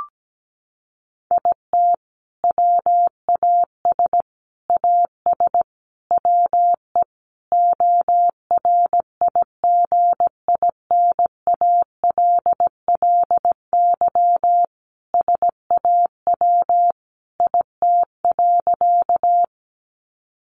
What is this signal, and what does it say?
Morse code, heard as a keyed tone, about 17 wpm, 700 Hz. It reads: IT WAS AS WE ORIGINALLY SAW IT.
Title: Sentences from Top 1500 Words 17wpm